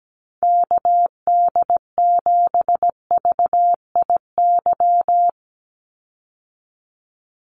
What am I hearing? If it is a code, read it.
Morse code, heard as a keyed tone, about 17 wpm, 700 Hz. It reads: KD7VIY
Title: USA Call Signs 17wpm